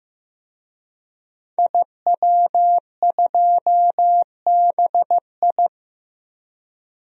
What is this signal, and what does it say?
Morse code, heard as a keyed tone, about 15 wpm, 700 Hz. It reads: IW2BI
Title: DXCC-100 Call Signs - 5 Characters - 15wpm